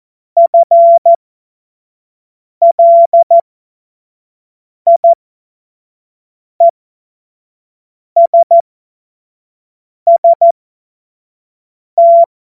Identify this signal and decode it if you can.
Morse code, heard as a keyed tone, about 14 wpm, spaced as for 5 wpm, 680 Hz. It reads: FLIESST
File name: Morsekurs / Witze / Witz 02. November 2021, 25 F70